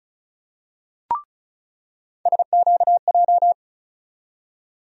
Morse code, heard as a keyed tone, about 36 wpm, 700 Hz. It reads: SQJ
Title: USA Call Sign Postfixes - 3 Letters 35wpm